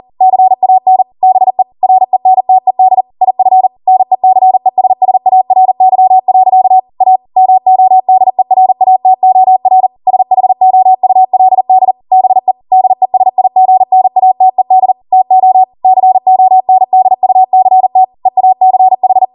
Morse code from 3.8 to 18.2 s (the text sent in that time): NECESSARY. A MODERATOR SHOULD BE DESIGNATED TO CONDUCT